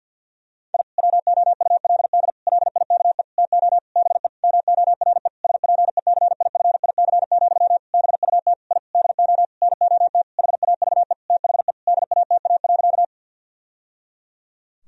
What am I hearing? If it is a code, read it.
Morse code, heard as a keyed tone, about 52 wpm, 700 Hz. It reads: I WOULD LIKE TO BE MORE SPECIFIC, BUT I DO NOT HAVE THE DATA.
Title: Sentences from Top 1000 Words - Review Part 3 - 50wpm